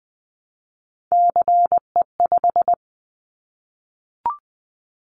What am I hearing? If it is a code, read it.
Morse code, heard as a keyed tone, about 20 wpm, 700 Hz. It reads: CE5
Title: Callsign Prefixes for Top 100 DXCC - 20wpm